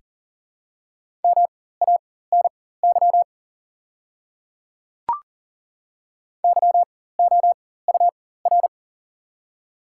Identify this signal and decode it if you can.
Morse code, heard as a keyed tone, about 40 wpm, spaced as for 19 wpm, 690 Hz. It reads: MANY YOUR
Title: 4x Character Spacing - Top 100 Words - 40wpm